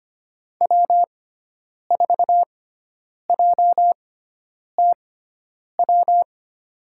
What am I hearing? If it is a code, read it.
Morse code, heard as a keyed tone, about 25 wpm, spaced as for 9 wpm, 700 Hz. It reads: W4JTW
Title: USA Call Signs - 5 Characters - 6x Character Spacing - 25wpm